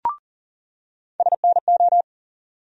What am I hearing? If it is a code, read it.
Morse code, heard as a keyed tone, about 40 wpm, 700 Hz. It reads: SNO
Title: USA Call Sign Postfixes - 3 Letters 40wpm